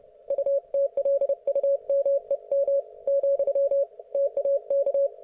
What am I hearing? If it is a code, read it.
Morse code, heard as a keyed tone, about 29 wpm, 560 Hz. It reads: UTLUMEM , TAK